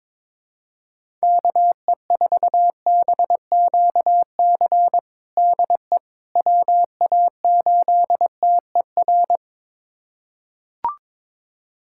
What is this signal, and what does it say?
Morse code, heard as a keyed tone, about 22 wpm, 700 Hz. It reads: KE4BQC DE WA8TER
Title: USA Callsign DE Callsign 22wpm